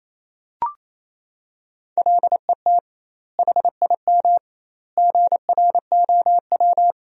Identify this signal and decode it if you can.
Morse code, heard as a keyed tone, about 27 wpm, 700 Hz. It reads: LET HIM GROW
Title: 2x Word-Spacing - Sentences from Top 300 Words - 28wpm